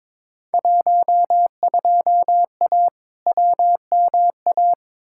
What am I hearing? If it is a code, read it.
Morse code, heard as a keyed tone, about 22 wpm, 700 Hz. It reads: 12A WMA
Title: Field Day Exchanges Encore - 22wpm